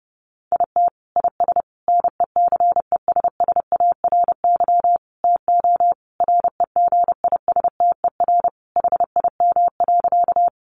Morse code, heard as a keyed tone, about 30 wpm, 700 Hz. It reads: IT IS NECESSARY TO REGISTER HIM.
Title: Sentences from Top 1200 Words 30wpm